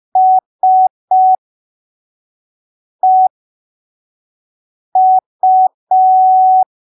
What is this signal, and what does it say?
Morse code, heard as a keyed tone, about 5 wpm, spaced as for 3 wpm, 750 Hz. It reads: SEU
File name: arquivos / texto / texto 20 05ppm